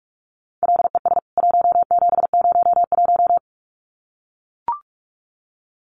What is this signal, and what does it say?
Morse code, heard as a keyed tone, about 44 wpm, 700 Hz. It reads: LES 1701
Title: CWT Exchange 45wpm